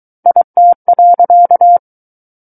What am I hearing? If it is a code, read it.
Morse code, heard as a keyed tone, about 23 wpm, 700 Hz. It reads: IT.
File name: Book One / 14-Chapter 14 - 23 WPM